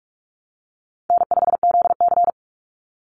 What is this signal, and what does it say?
Morse code, heard as a keyed tone, about 44 wpm, 700 Hz. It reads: N5ZC
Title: Easy Call Signs Repeated 2x - 45wpm